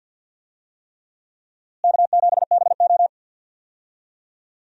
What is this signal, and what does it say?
Morse code, heard as a keyed tone, about 48 wpm, 700 Hz. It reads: K7BO
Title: USA Call Signs - 4 Characters - Repeated 3x - 50wpm